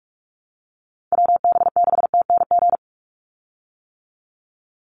Morse code, heard as a keyed tone, about 44 wpm, 700 Hz. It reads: WB6TNG
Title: USA Call Signs 45wpm